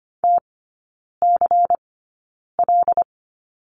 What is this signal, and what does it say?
Morse code, heard as a keyed tone, about 25 wpm, spaced as for 9 wpm, 700 Hz. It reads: TCL
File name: course / 25:10wpm / 093 Lesson 16e - Letter c - Three Characters 25wpm